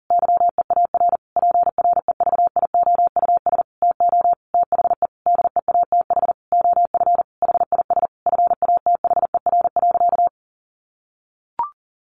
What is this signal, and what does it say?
Morse code, heard as a keyed tone, about 40 wpm, 700 Hz. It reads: YEAR PREVIOUS TO THE DEATH OF HIS FATHER.
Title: Sentences from Top 1500 Words 40wpm